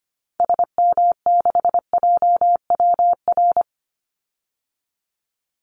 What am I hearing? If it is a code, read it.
Morse code, heard as a keyed tone, about 25 wpm, 700 Hz. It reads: SM6JWR
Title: Call Signs Repeated 3x - 25wpm